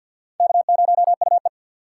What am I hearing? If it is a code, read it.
Morse code, heard as a keyed tone, about 47 wpm, 700 Hz. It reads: K0UE
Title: USA Call Signs - 4 Characters - 3x Character Spacing - 50wpm